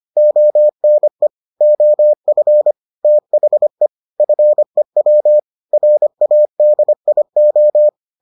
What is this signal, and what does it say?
Morse code, heard as a keyed tone, about 25 wpm, 600 Hz. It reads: ONE OF THE FEW RADIO